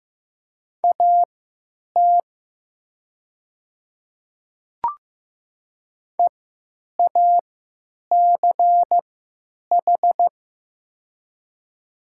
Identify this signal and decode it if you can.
Morse code, heard as a keyed tone, about 15 wpm, spaced as for 9 wpm, 700 Hz. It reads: AT EACH
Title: 3x Character Spacing - Top 100 Words - 15wpm